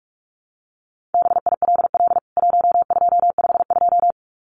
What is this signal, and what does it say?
Morse code, heard as a keyed tone, about 44 wpm, 700 Hz. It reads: BILL 1252
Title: CWT Exchange 45wpm